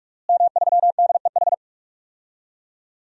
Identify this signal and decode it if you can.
Morse code, heard as a keyed tone, about 43 wpm, 700 Hz. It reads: M3DEH
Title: DXCC-100 Call Signs - 5 Characters - Repeated 2x - 45wpm